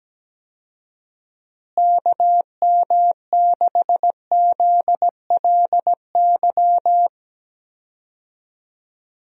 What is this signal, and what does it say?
Morse code, heard as a keyed tone, about 17 wpm, 700 Hz. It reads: KM6ZLY
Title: USA Call Signs - 6 Characters - Repeated 2x - 17wpm